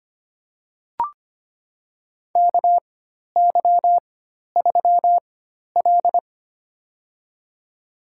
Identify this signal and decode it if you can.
Morse code, heard as a keyed tone, about 25 wpm, spaced as for 12 wpm, 700 Hz. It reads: KY3L
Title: USA Call Signs - 4 Characters - 4x Character Spacing - 25wpm